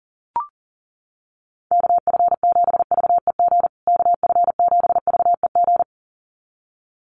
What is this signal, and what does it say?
Morse code, heard as a keyed tone, about 40 wpm, 700 Hz. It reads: KF7VEG KF7VEG
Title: USA Call Signs Repeated 2x 40wpm